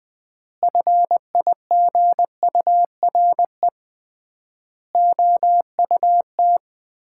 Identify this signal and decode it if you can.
Morse code, heard as a keyed tone, about 20 wpm, 700 Hz. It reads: FIGURE OUT
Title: 3x Word-Spacing - Sets of 5 Words - 20wpm